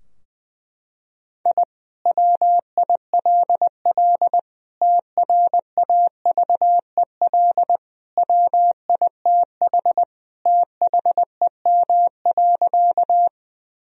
Morse code, heard as a keyed tone, about 20 wpm, 700 Hz. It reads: I WILL TRAVEL WITH THEM.
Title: Sentences from Top 1000 Words - Review Part 2 - 20wpm